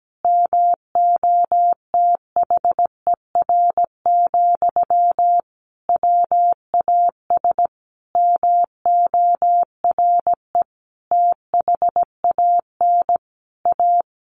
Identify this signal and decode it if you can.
Morse code, heard as a keyed tone, about 17 wpm, 700 Hz. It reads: MOTHER, WAS MORE THAN A
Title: Sentences from Top 1500 Words 17wpm